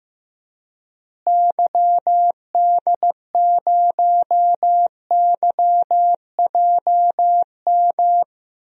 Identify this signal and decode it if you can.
Morse code, heard as a keyed tone, about 15 wpm, 700 Hz. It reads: YD0YJM
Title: DXCC-100 Call Signs - 6 Characters - Repeated 3x - 15wpm